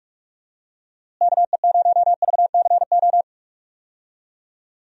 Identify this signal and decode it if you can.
Morse code, heard as a keyed tone, about 43 wpm, 700 Hz. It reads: KE0VCO
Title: USA Call Signs - 6 Characters - Repeated 2x - 45wpm